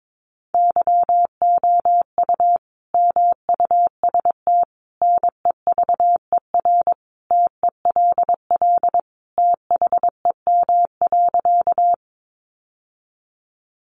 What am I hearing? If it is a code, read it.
Morse code, heard as a keyed tone, about 22 wpm, 700 Hz. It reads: YOU MUST NEVER TELL THEM.
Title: Sentences from Top 900 Words 22wpm